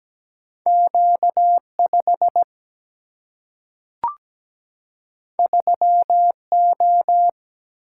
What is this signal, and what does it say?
Morse code, heard as a keyed tone, about 17 wpm, 700 Hz. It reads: Q5 3O
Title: Single-Letter Single-Number 17wpm